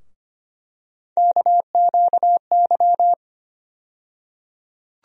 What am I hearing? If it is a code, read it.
Morse code, heard as a keyed tone, about 25 wpm, 700 Hz. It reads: KQY
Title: USA Call Sign Postfixes - 3 Letters 25wpm